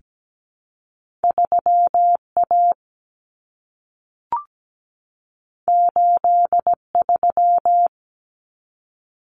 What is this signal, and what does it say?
Morse code, heard as a keyed tone, about 17 wpm, 700 Hz. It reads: 3A 83
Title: Common Cut Numbers - 2-Digit 17wpm